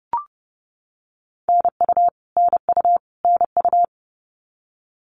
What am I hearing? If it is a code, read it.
Morse code, heard as a keyed tone, about 30 wpm, 700 Hz. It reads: NU NU NU
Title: Canadian Provinces and Territories - Repeated 3x - 30wpm